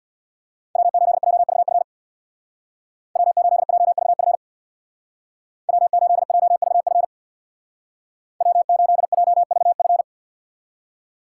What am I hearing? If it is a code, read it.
Morse code, heard as a keyed tone, about 62 wpm, 700 Hz. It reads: W8JVF W8JVF W8JVF W8JVF
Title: Speed Racing - USA 1x3 Call Signs 50wpm